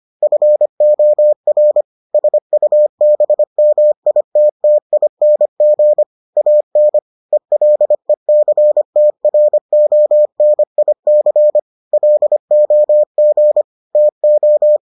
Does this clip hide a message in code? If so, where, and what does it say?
Yes: Morse code from 0.2 to 14.8 s, heard as a keyed tone, about 25 wpm, 600 Hz. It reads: FOR SUBMITTING AN ELECTRONIC LOG TO